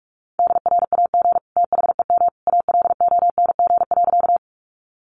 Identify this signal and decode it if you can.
Morse code, heard as a keyed tone, about 44 wpm, 700 Hz. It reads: DRAG THEM ALONG.
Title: Sentences from Top 1500 Words 45wpm